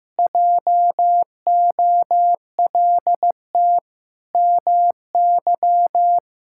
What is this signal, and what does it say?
Morse code, heard as a keyed tone, about 15 wpm, 700 Hz. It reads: JOLT MY
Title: Pangram Challenge v2 - 15wpm